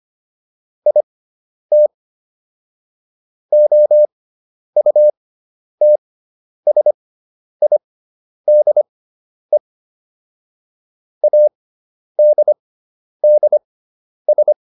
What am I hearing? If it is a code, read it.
Morse code, heard as a keyed tone, about 25 wpm, spaced as for 10 wpm, 600 Hz. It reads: IT OUTSIDE ADDS